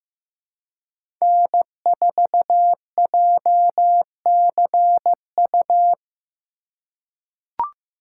Morse code, heard as a keyed tone, about 15 wpm, 700 Hz. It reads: N4JCU
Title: USA Call Signs - 4x Character Spacing - 15wpm